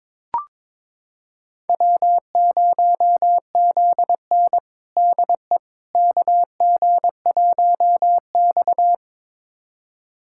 Morse code, heard as a keyed tone, about 22 wpm, 700 Hz. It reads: W0ZN DE KG1X
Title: USA Callsign DE Callsign - 4 Characters - 22wpm